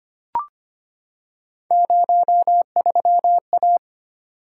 Morse code, heard as a keyed tone, about 25 wpm, 700 Hz. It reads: 03A
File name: Three Letter-Number Combinations - 25wpm